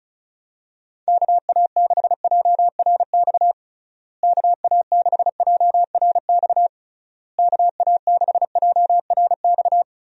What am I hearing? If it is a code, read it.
Morse code, heard as a keyed tone, about 36 wpm, 700 Hz. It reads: KA6JRX KA6JRX KA6JRX
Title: USA Call Signs - 6 Characters - Repeated 3x - 35wpm